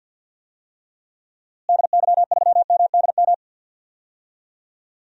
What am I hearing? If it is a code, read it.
Morse code, heard as a keyed tone, about 45 wpm, 700 Hz. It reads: DY3MDM
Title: DXCC-100 Call Signs - 6 Characters - Repeated 2x - 50wpm